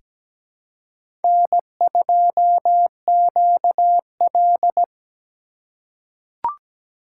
Morse code, heard as a keyed tone, about 17 wpm, 700 Hz. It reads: N2QL
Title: USA Call Signs - 4 Characters - 3x Character Spacing - 17wpm